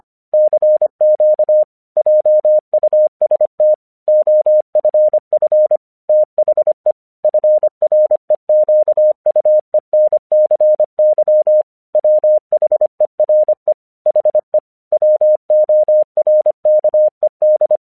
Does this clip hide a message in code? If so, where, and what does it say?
Yes: Morse code from 0.3 to 17.8 s, heard as a keyed tone, about 24 wpm, 610 Hz. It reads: CQ JUST OFF THE FREQUENCY WHERE HE WORKED